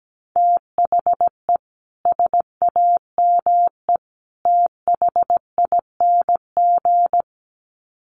Morse code, heard as a keyed tone, about 17 wpm, 700 Hz. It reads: THE SAME THING
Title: Sets of 3 Words 17wpm